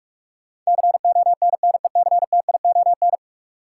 Morse code, heard as a keyed tone, about 42 wpm, 700 Hz. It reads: CONNECTION
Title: Sentences from Top 1300 Words - New Word Drill Alt - 45wpm